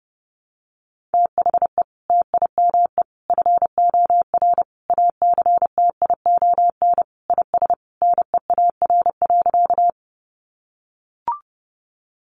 Morse code, heard as a keyed tone, about 30 wpm, 700 Hz. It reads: THE TIME FOR ACTION IS NEAR.
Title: Sentences from Top 700 Words 30wpm